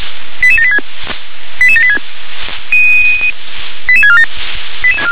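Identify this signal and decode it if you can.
Five-tone selective call (ZVEI), heard as quick runs of five tones, tones of about 70 ms. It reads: 88786 88786 88647
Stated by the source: ZVEI five-tone selective call received with an SDR.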